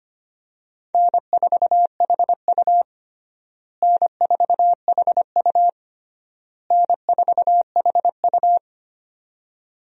Morse code, heard as a keyed tone, about 25 wpm, 700 Hz. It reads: N4HU N4HU N4HU
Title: USA Call Signs - 4 Characters - Repeated 3x - 25wpm